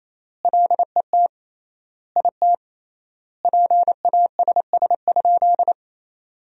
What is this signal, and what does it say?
Morse code, heard as a keyed tone, about 28 wpm, 700 Hz. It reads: LET IT PASS?
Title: 3x Word-Spacing - Sentences from Top 400 Words - 28wpm